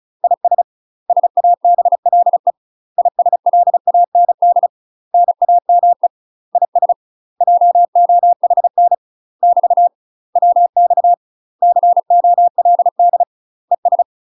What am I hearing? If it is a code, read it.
Morse code, heard as a keyed tone, about 34 wpm, 700 Hz. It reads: IS SABLE ISLAND NAME IS JOHN <BT> WX COLD ES
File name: UK MASTER QSOs TRAINING / 35WPM / UK MASTER QSOs 0003